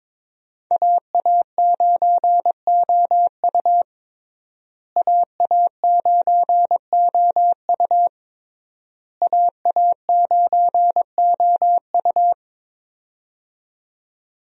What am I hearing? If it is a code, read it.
Morse code, heard as a keyed tone, about 22 wpm, 700 Hz. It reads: AA9OU AA9OU AA9OU
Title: USA Call Signs - 5 Characters - Repeated 3x - 22wpm